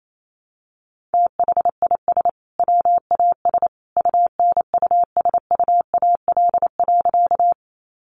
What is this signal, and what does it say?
Morse code, heard as a keyed tone, about 28 wpm, 700 Hz. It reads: THIS WAS UNUSUAL.
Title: Sentences from Top 1400 Words 28wpm